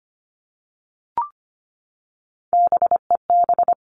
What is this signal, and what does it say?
Morse code, heard as a keyed tone, about 25 wpm, 700 Hz. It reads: BEB
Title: All Cut Numbers - 3-Digit 25wpm